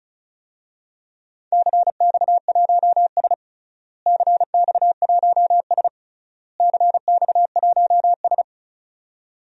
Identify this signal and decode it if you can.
Morse code, heard as a keyed tone, about 36 wpm, 700 Hz. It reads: CX1S CX1S CX1S
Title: DXCC-100 Call Signs - 4 Characters - Repeated 3x - 35wpm